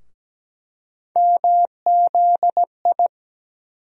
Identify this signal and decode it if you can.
Morse code, heard as a keyed tone, about 17 wpm, 700 Hz. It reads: MZI
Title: USA Call Sign Postfixes - 3 Letters 17wpm